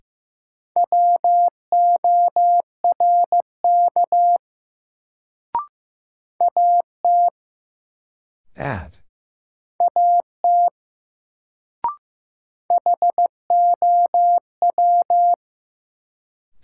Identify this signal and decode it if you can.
Morse code, heard as a keyed tone, about 15 wpm, 700 Hz. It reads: WORK AT AT HOW
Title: Sets of 2 Words - New Word Drill - 15wpm